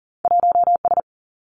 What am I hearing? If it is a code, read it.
Morse code, heard as a keyed tone, about 41 wpm, 700 Hz. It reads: 1 S
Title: Two Challenging Letter-Number Combinations 40wpm